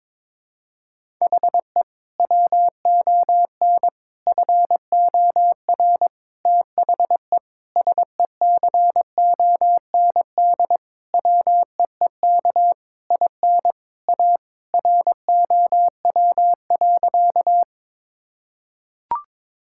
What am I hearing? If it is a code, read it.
Morse code, heard as a keyed tone, about 22 wpm, 700 Hz. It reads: HE WON FOR THE SECOND WEEK IN A ROW.
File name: practice / 22wpm / Sentences with 7-Word Phrases - 22wpm